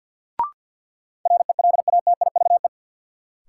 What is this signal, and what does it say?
Morse code, heard as a keyed tone, about 53 wpm, 700 Hz. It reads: RELATIVE
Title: Sentences from Top 1400 Words - New Word Drill - 50wpm